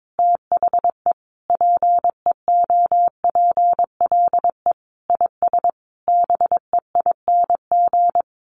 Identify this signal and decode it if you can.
Morse code, heard as a keyed tone, about 22 wpm, 700 Hz. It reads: THE PEOPLE IS BEING
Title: Sentences from Top 1500 Words 22wpm